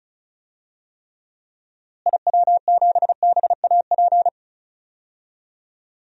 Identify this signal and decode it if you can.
Morse code, heard as a keyed tone, about 36 wpm, 700 Hz. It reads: IW7BAP
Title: DXCC-100 Call Signs - 6 Characters - Repeated 3x - 35wpm